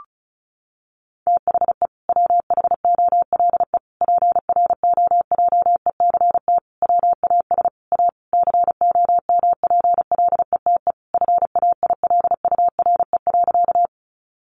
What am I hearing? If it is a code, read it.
Morse code, heard as a keyed tone, about 34 wpm, 700 Hz. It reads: THE WHOLE PROJECT WAS A COMPLETE FAILURE.
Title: Sentences from Top 1400 Words 35wpm